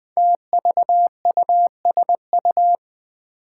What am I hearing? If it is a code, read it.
Morse code, heard as a keyed tone, about 20 wpm, 700 Hz. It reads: TVUSU
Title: Five Challenging Letter-Number Combinations 20wpm